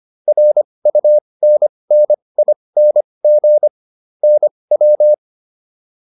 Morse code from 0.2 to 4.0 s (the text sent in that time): RUNNING